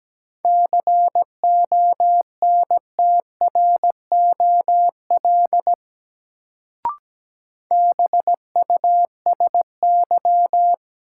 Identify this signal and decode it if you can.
Morse code, heard as a keyed tone, about 17 wpm, 700 Hz. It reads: CONTROL BUSY
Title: Sentences from Top 1000 Words - Review Part 2 - New Word Drill - 17wpm